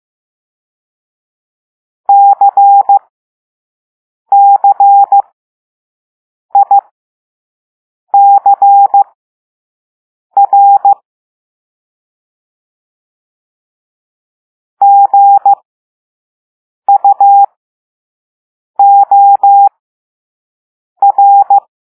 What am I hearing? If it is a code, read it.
Morse code, heard as a keyed tone, about 15 wpm, spaced as for 5 wpm, 800 Hz. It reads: CCICR GUOR
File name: Morse / Level6-All-A